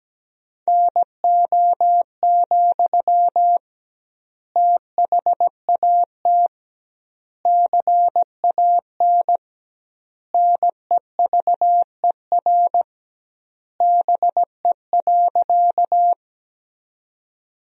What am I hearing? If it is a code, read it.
Morse code, heard as a keyed tone, about 17 wpm, 700 Hz. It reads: NO, THAT CAN NEVER BE.
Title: 2x Word-Spacing - Sentences from Top 300 Words - 17wpm